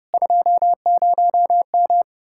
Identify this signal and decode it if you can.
Morse code, heard as a keyed tone, about 37 wpm, 700 Hz. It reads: 20M
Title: Speed-Racing Common QSO Phrases Encore - 30wpm